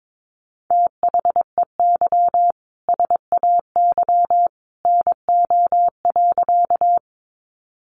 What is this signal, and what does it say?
Morse code, heard as a keyed tone, about 22 wpm, 700 Hz. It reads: THEY SAY NO.
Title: Easy Phrases 22wpm